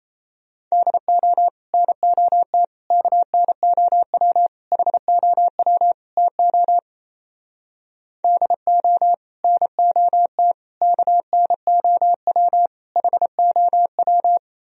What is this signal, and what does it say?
Morse code, heard as a keyed tone, about 38 wpm, 700 Hz. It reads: DO NOT KNOW HOW TO DO NOT KNOW HOW